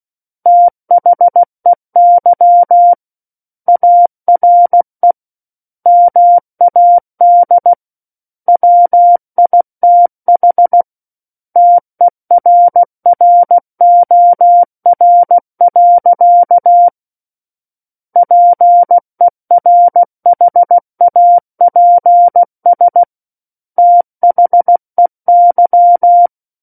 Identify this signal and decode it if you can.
Morse code, heard as a keyed tone, about 16 wpm, 700 Hz. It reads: THEY ARE MAD WITH TERROR. PERHAPS THEY
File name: Book One / 07-Chapter 7 - 16 WPM